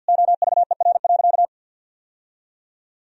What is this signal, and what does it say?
Morse code, heard as a keyed tone, about 50 wpm, 700 Hz. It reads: OVER.
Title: Sentences from Top 1000 Words - Review Part 4 - 50wpm